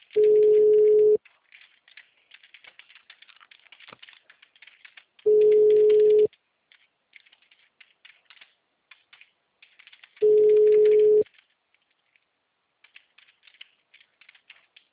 Ringback tone (Europe, 425 Hz).
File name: out- -2012- - - .44979